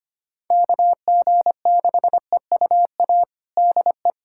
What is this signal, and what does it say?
Morse code, heard as a keyed tone, about 25 wpm, 700 Hz. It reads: KG6EUA DE
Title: USA Callsign DE Callsign - 6 Characters - 25wpm